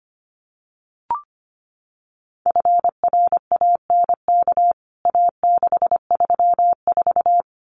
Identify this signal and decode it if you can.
Morse code, heard as a keyed tone, about 25 wpm, 700 Hz. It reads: FRANK A634
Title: CWT Exchange 25wpm